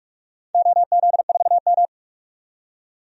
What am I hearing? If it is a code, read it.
Morse code, heard as a keyed tone, about 43 wpm, 700 Hz. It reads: OZ4M
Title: DXCC-100 Call Signs - 4 Characters - 45wpm